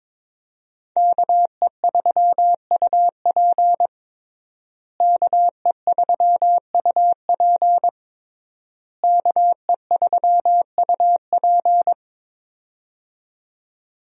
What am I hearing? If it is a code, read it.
Morse code, heard as a keyed tone, about 22 wpm, 700 Hz. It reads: KE3UP KE3UP KE3UP
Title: USA Call Signs - 5 Characters - Repeated 3x - 22wpm